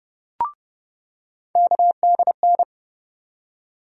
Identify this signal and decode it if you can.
Morse code, heard as a keyed tone, about 32 wpm, 700 Hz. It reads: KDN
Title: USA Call Sign Postfixes - 3 Letters 30wpm